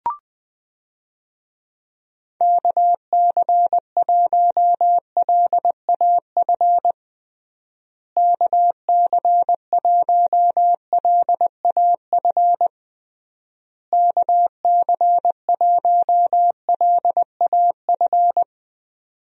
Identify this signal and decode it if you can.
Morse code, heard as a keyed tone, about 20 wpm, 700 Hz. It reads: KC1LAF KC1LAF KC1LAF
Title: USA Call Signs - 6 Characters - Repeated 3x - 20wpm